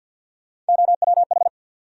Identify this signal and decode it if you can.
Morse code, heard as a keyed tone, about 51 wpm, 700 Hz. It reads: OWH
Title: USA Call Sign Postfixes - 3 Letters 50wpm